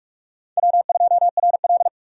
Morse code, heard as a keyed tone, about 46 wpm, 700 Hz. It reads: W2RL
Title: USA Call Signs - 4 Characters - 45wpm